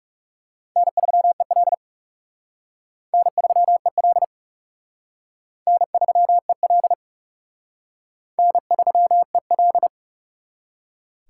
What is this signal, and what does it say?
Morse code, heard as a keyed tone, about 37 wpm, 700 Hz. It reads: N3EL N3EL N3EL N3EL
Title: Speed Racing - USA 1x2 Call Signs 30wpm